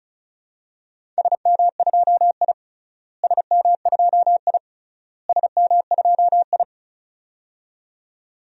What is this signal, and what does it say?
Morse code, heard as a keyed tone, about 36 wpm, 700 Hz. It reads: SM2I SM2I SM2I
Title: DXCC-100 Call Signs - 4 Characters - Repeated 3x - 35wpm